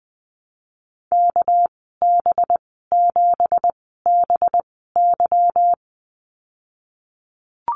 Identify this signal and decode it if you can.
Morse code, heard as a keyed tone, about 20 wpm, spaced as for 15 wpm, 700 Hz. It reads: KB7BY
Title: 200 Lesson 32g - Letter k - Call Signs 20wpm